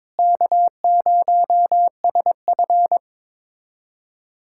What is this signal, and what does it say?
Morse code, heard as a keyed tone, about 22 wpm, 700 Hz. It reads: K0SF
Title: USA Callsign DE Callsign - 4 Characters - 22wpm